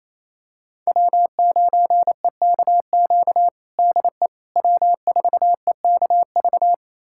Morse code, heard as a keyed tone, about 27 wpm, 700 Hz. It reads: W9EKQ DE W4EKV
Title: USA Callsign DE Callsign - 5 Characters - 28wpm